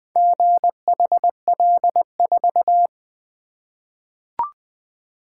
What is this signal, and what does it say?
Morse code, heard as a keyed tone, about 20 wpm, 700 Hz. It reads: GHL4
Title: Four Letter-Number Combinations - 20wpm